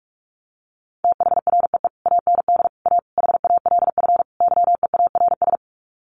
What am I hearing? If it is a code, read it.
Morse code, heard as a keyed tone, about 44 wpm, 700 Hz. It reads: THREE AND A HALF YEARS
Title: Sets of 5 Words Encore2 45wpm